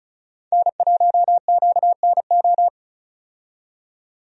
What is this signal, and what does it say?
Morse code, heard as a keyed tone, about 34 wpm, 700 Hz. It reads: N1QNO
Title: USA Call Signs - 5 Characters - 35wpm